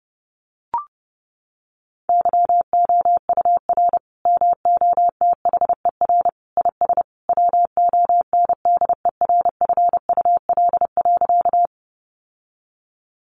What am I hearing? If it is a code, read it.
Morse code, heard as a keyed tone, about 30 wpm, 700 Hz. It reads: YOUR MOTHER IS WONDERFUL.
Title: Sentences from Top 1100 Words 30wpm